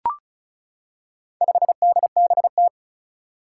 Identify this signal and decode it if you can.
Morse code, heard as a keyed tone, about 36 wpm, 700 Hz. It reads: 5DBT
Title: Four Challenging Letter-Number Combinations 35wpm